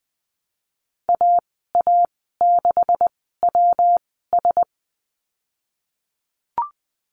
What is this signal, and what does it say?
Morse code, heard as a keyed tone, about 20 wpm, spaced as for 15 wpm, 700 Hz. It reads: AA6WS
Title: 154 Lesson 25g - Letter p - Call Signs 20wpm